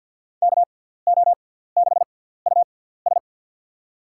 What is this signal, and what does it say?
Morse code, heard as a keyed tone, about 45 wpm, spaced as for 17 wpm, 700 Hz. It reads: KO6US
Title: USA Call Signs - 6x Character Spacing - 50wpm